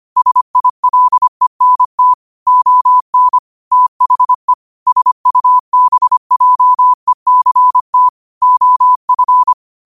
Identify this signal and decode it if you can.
Morse code, heard as a keyed tone, about 25 wpm, 1000 Hz. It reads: SILENT ON THE SUBJECT OF
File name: qotdfiles / 25wpm